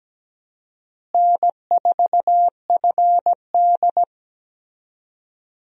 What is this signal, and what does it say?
Morse code, heard as a keyed tone, about 17 wpm, 700 Hz. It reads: N4FD
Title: USA Call Signs - 4 Characters - Repeated 2x - 17wpm